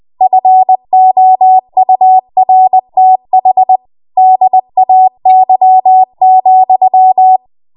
Morse code, heard as a keyed tone, about 20 wpm, 750 Hz. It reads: FOURTH DAY,
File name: Morse / 20WPM